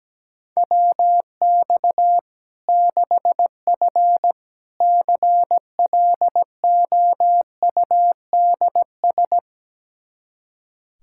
Morse code, heard as a keyed tone, about 17 wpm, 700 Hz. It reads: WX 6F CLOUDS
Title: Practice QSO - Exchange 2 - Info - 17wpm